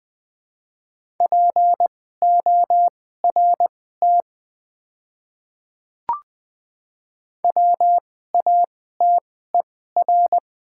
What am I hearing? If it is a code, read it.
Morse code, heard as a keyed tone, about 20 wpm, spaced as for 14 wpm, 700 Hz. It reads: PORT WATER
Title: 2x Character Spacing - Top 200 Words - 20wpm